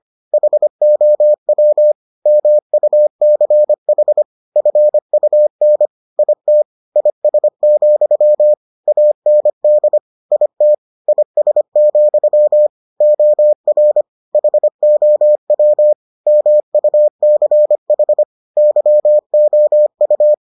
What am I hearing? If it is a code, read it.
Morse code, heard as a keyed tone, about 25 wpm, 600 Hz. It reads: HOW MUCH FUN IT IS, AND IT IS, OR HOW MUCH YOU